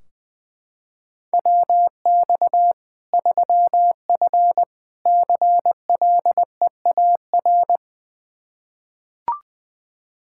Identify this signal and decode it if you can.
Morse code, heard as a keyed tone, about 21 wpm, 700 Hz. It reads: WX 3F CLEAR
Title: K1USN SST QSO Practice 20wpm